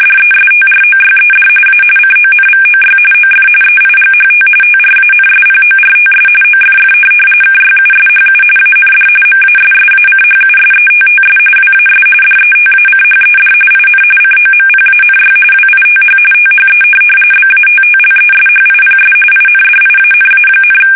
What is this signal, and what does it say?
RTTY, about 45.45 baud. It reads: DESTROYED A CONVOY OF EIGHT TRANSPORTS OFF KYUSHU AND MASSACHUSETTS WAS DETACHED IN COMPANY WITH THE BATTLESHIPS INDIANA,
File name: fsk850 / USS-Massachusetts